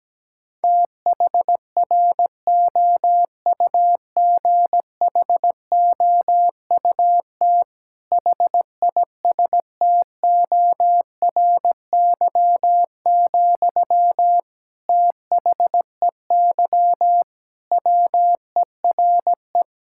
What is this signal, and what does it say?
Morse code, heard as a keyed tone, about 17 wpm, 700 Hz. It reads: THROUGHOUT HISTORY, THEY WERE
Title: Sentences from Top 1000 Words - Review Part 5 - 17wpm